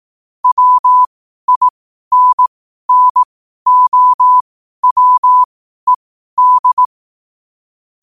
Morse code, heard as a keyed tone, about 18 wpm, spaced as for 12 wpm, 1000 Hz. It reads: WINNOWED
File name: qotdfiles / 12wpm